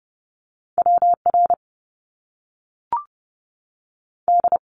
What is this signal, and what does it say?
Morse code, heard as a keyed tone, about 30 wpm, spaced as for 3 wpm, 700 Hz. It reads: W RD
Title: Top Two Letter Combinations in Words 30wpm